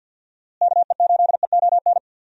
Morse code, heard as a keyed tone, about 51 wpm, 700 Hz. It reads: KE8EON
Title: USA Call Signs - 6 Characters - Repeated 2x - 50wpm